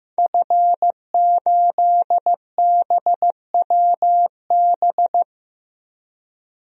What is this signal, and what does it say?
Morse code, heard as a keyed tone, about 15 wpm, 700 Hz. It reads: F8BWB
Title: DXCC-100 Call Signs - 5 Characters - 15wpm